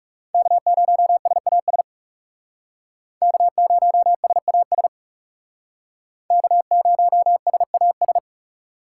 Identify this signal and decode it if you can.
Morse code, heard as a keyed tone, about 38 wpm, 700 Hz. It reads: K0SAS K0SAS K0SAS